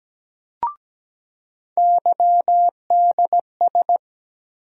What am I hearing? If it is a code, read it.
Morse code, heard as a keyed tone, about 17 wpm, 700 Hz. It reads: YDS
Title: USA Call Sign Postfixes - 3 Letters 17wpm